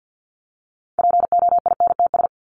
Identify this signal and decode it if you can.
Morse code, heard as a keyed tone, about 48 wpm, 700 Hz. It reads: POINTS
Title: ES:S Suffix - Top 100 Words with Base Words 50wpm